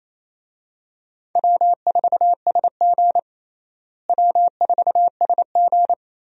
Morse code, heard as a keyed tone, about 28 wpm, 700 Hz. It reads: W4SG W4SG
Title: USA Call Signs - 4 Characters - Repeated 2x - 28wpm